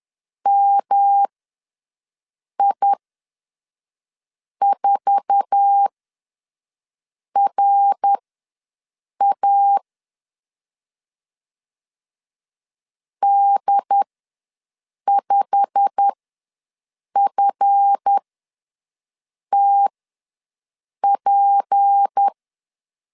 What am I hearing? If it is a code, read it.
Morse code, heard as a keyed tone, about 10 wpm, spaced as for 5 wpm, 800 Hz. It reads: MI4RA D5FTP
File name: LEZIONI / 8 YJ45 / 8 YJ45 ws 10wpm 3843